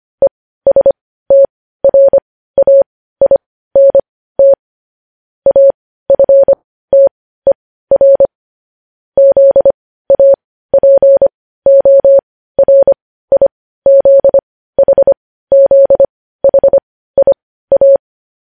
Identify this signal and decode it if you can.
Morse code, heard as a keyed tone, about 25 wpm, spaced as for 15 wpm, 550 Hz. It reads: ESTRAINT AFTER ZAPORIZHZHIA